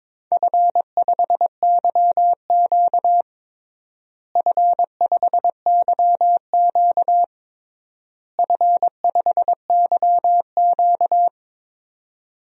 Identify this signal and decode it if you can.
Morse code, heard as a keyed tone, about 22 wpm, 700 Hz. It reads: F5YQ F5YQ F5YQ
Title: DXCC-100 Call Signs - 4 Characters - Repeated 3x - 22wpm